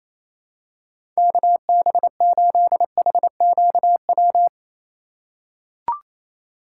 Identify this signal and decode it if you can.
Morse code, heard as a keyed tone, about 28 wpm, 700 Hz. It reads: KB8HQW
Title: USA Call Signs - 6 Characters - 4x Character Spacing - 28wpm